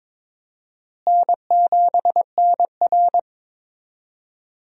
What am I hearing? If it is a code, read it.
Morse code, heard as a keyed tone, about 21 wpm, 700 Hz. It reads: N7NR
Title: USA Call Signs - 4 Characters - Repeated 2x - 22wpm